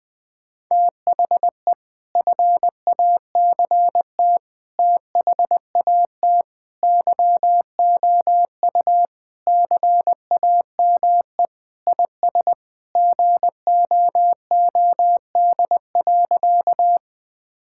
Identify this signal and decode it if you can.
Morse code, heard as a keyed tone, about 20 wpm, 700 Hz. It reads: THE FACT THAT YOU CAME IS GOOD.
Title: Sentences with 3-Word Phrases - Spoken Beginning - 20wpm